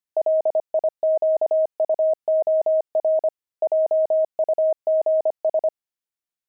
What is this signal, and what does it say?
Morse code, heard as a keyed tone, about 25 wpm, 630 Hz. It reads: LIQUOR JUGS
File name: pangram-fs-25-cs-25-jugs